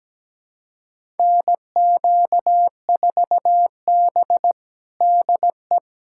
Morse code, heard as a keyed tone, about 17 wpm, 700 Hz. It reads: NQ4B DE
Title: USA Callsign DE Callsign - 4 Characters - 17wpm